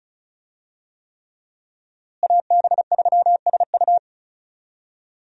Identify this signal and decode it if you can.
Morse code, heard as a keyed tone, about 36 wpm, 700 Hz. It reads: AB3SU
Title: USA Call Signs - 5 Characters - Repeated 3x - 35wpm